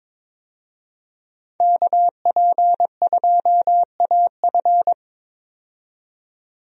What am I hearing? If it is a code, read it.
Morse code, heard as a keyed tone, about 22 wpm, 700 Hz. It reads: KP2AF
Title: USA Call Signs - 5 Characters - 22wpm